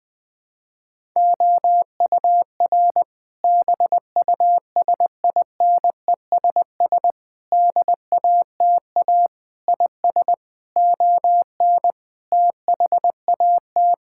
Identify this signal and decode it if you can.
Morse code, heard as a keyed tone, about 20 wpm, 700 Hz. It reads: OUR BUSINESS DATA IS ON THAT
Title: Sentences from Top 1000 Words - Review Part 6 - 20wpm